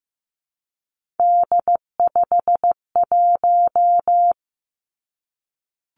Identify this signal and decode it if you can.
Morse code, heard as a keyed tone, about 15 wpm, 700 Hz. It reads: D51
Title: Three Challenging Letter-Number Combinations 15wpm